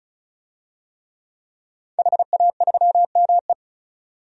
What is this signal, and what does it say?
Morse code, heard as a keyed tone, about 36 wpm, 700 Hz. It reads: HA3ME
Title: DXCC-100 Call Signs - 5 Characters - 35wpm